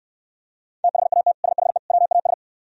Morse code, heard as a keyed tone, about 67 wpm, 700 Hz. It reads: THAT SHE WAS